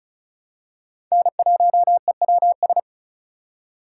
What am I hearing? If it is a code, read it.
Morse code, heard as a keyed tone, about 36 wpm, 700 Hz. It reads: N1EWS
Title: USA Call Signs - 5 Characters - 4x Character Spacing - 35wpm